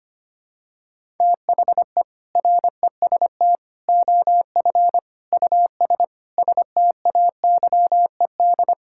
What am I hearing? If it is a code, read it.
Morse code, heard as a keyed tone, about 25 wpm, 700 Hz. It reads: THE REST OF US STAYED
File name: practice / 25wpm / Sentences with 4-Word Phrases - 25wpm